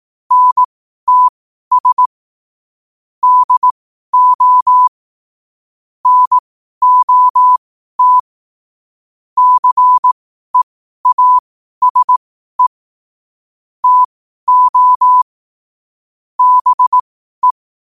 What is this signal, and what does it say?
Morse code, heard as a keyed tone, about 18 wpm, spaced as for 12 wpm, 1000 Hz. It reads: NTS DO NOT CEASE TO BE
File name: qotdfiles / 12wpm